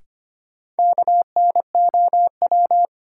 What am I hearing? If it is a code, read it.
Morse code, heard as a keyed tone, about 25 wpm, 700 Hz. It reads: KNOW
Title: Sentences from Top 100 Words - New Word Drill - 25wpm